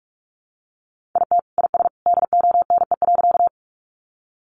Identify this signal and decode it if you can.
Morse code, heard as a keyed tone, about 44 wpm, 700 Hz. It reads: IT IS DONE.
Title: Sentences from Top 500 Words 45wpm